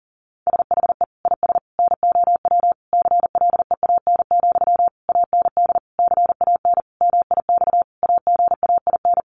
Morse code, heard as a keyed tone, about 40 wpm, 700 Hz. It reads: SHE IS NOW CLEAN, AND CAN MIX AGAIN
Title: Sentences from Top 1100 Words 40wpm